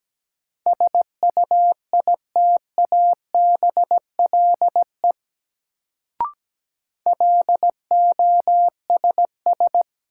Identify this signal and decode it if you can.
Morse code, heard as a keyed tone, about 17 wpm, 700 Hz. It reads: SUITABLE LOSS
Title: Sentences from Top 1400 Words - New Word Drill - 17wpm